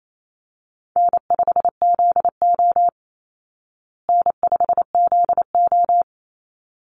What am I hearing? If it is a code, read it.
Morse code, heard as a keyed tone, about 28 wpm, 700 Hz. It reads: N5ZO N5ZO
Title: Easy Call Signs Repeated 2x - 28wpm